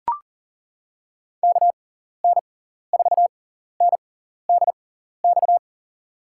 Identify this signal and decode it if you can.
Morse code, heard as a keyed tone, about 40 wpm, spaced as for 14 wpm, 700 Hz. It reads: KN4NDX
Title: USA Call Signs - 6 Characters - 6x Character Spacing - 40wpm